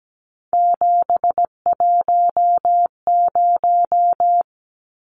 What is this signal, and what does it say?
Morse code, heard as a keyed tone, about 17 wpm, 700 Hz. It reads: 710
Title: Numbers - 3-Digit 17wpm